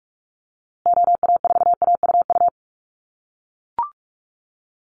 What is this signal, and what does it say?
Morse code, heard as a keyed tone, about 44 wpm, 700 Hz. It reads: OA4AUU
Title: USA Call Signs Repeated 2x 45wpm